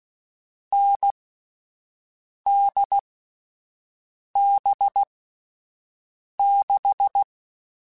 Morse code, heard as a keyed tone, about 16 wpm, spaced as for 5 wpm, 800 Hz. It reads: NDB6
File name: Difficult Combos / Difficult Combos-mp3s / awj1 ndb6 gz7 @5